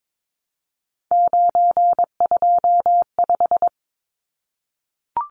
Morse code, heard as a keyed tone, about 22 wpm, 700 Hz. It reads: 925
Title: Numbers - 3-Digit 22wpm